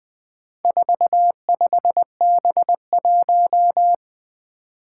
Morse code, heard as a keyed tone, about 20 wpm, 700 Hz. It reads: 45B1
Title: Four Challenging Letter-Number Combinations 20wpm